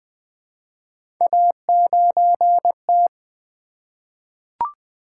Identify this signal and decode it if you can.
Morse code, heard as a keyed tone, about 20 wpm, 700 Hz. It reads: A9T
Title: Three Letter-Number Combinations - 20wpm